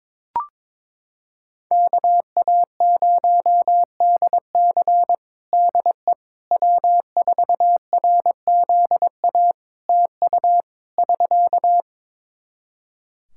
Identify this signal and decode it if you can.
Morse code, heard as a keyed tone, about 21 wpm, 700 Hz. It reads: KA0DC DE W4RZA TU <SK>
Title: Practice QSO - Exchange 4 - Ending - 22wpm